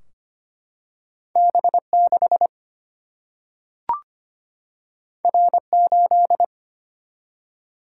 Morse code, heard as a keyed tone, about 25 wpm, 700 Hz. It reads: B6 R8
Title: Single-Letter Single-Number 25wpm